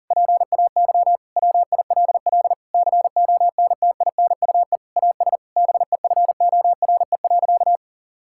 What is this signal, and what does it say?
Morse code, heard as a keyed tone, about 40 wpm, 700 Hz. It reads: PAY WILL CONTINUE AS BEFORE.
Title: Sentences from Top 1000 Words - Review Part 6 - 40wpm